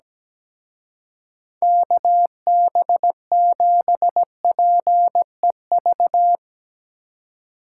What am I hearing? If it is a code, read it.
Morse code, heard as a keyed tone, about 17 wpm, 700 Hz. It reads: KB7PEV
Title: Speed Racing - USA 2x3 Call Signs 17wpm